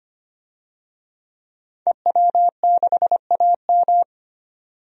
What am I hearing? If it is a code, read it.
Morse code, heard as a keyed tone, about 25 wpm, 700 Hz. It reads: EW6AM
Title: DXCC-100 Call Signs - 5 Characters - 25wpm